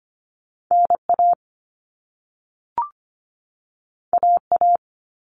Morse code, heard as a keyed tone, about 25 wpm, 700 Hz. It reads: N A A A
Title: All Cut Numbers - 2-Digit 25wpm